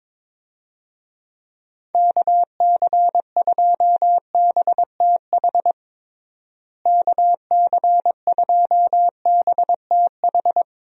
Morse code, heard as a keyed tone, about 22 wpm, 700 Hz. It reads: KC2BTH KC2BTH
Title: USA Call Signs - 6 Characters - Repeated 2x - 22wpm